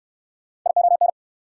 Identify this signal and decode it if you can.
Morse code, heard as a keyed tone, about 69 wpm, 690 Hz. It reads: ION